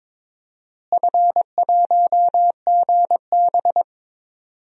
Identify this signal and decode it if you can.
Morse code, heard as a keyed tone, about 22 wpm, 700 Hz. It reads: F1GB
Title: DXCC-100 Call Signs - 4 Characters - Repeated 2x - 22wpm